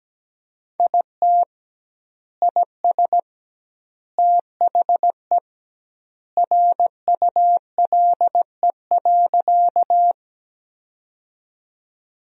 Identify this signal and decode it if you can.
Morse code, heard as a keyed tone, about 17 wpm, 700 Hz. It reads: IT IS THE RULE.
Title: 2x Word-Spacing - Sentences from Top 400 Words - 17wpm